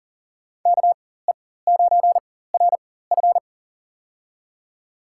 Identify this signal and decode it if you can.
Morse code, heard as a keyed tone, about 40 wpm, spaced as for 19 wpm, 700 Hz. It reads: KE9RF
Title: USA Call Signs - 5 Characters - 4x Character Spacing - 40wpm